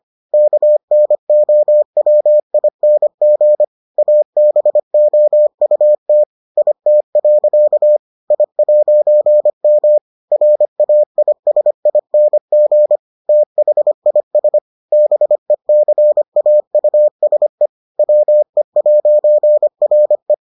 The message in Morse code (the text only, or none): KNOWING ABOUT IT. I'M RAISING THIS BECAUSE WE'RE